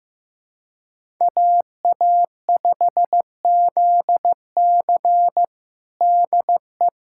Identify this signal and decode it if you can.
Morse code, heard as a keyed tone, about 15 wpm, 700 Hz. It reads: AA5ZC DE
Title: USA Callsign DE Callsign - 5 Characters - 15wpm